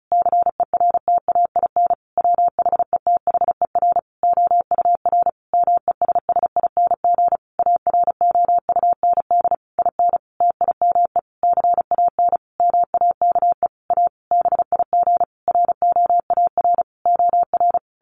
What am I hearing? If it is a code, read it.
Morse code, heard as a keyed tone, about 34 wpm, 700 Hz. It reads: CERTAIN WHETHER OUR MESSING AROUND IN TIME CAN MAKE A BIG ROAR OR
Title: A Sound Of Thunder 35wpm